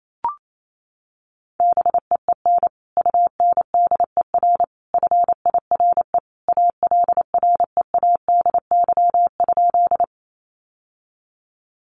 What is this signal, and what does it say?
Morse code, heard as a keyed tone, about 28 wpm, 700 Hz. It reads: BEEN UNDER FIRE ALREADY?
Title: Sentences from Top 700 Words 28wpm